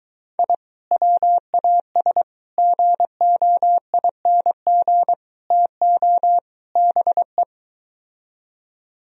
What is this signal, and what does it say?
Morse code, heard as a keyed tone, about 23 wpm, 700 Hz. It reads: I WAS GOING TO BE